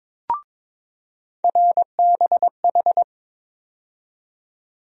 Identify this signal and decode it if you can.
Morse code, heard as a keyed tone, about 22 wpm, 700 Hz. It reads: RBH
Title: USA Call Sign Postfixes - 3 Letters 22wpm